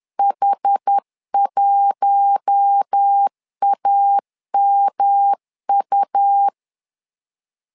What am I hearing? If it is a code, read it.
Morse code, heard as a keyed tone, about 10 wpm, 800 Hz. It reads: H1AMU
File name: LEZIONI / 4 Q1 10wpm 2448